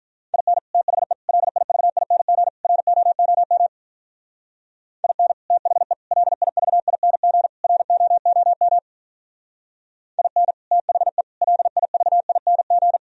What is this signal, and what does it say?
Morse code, heard as a keyed tone, about 44 wpm, 700 Hz. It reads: IN THE LIVING ROOM IN THE LIVING ROOM IN THE LIVING